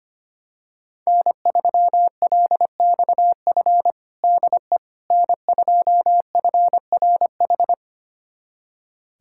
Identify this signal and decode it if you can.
Morse code, heard as a keyed tone, about 25 wpm, 700 Hz. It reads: N3LXF DE N2FRH
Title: USA Callsign DE Callsign - 5 Characters - 25wpm